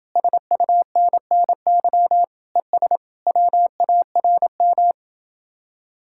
Morse code, heard as a keyed tone, about 31 wpm, 700 Hz. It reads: SUNNY ES WARM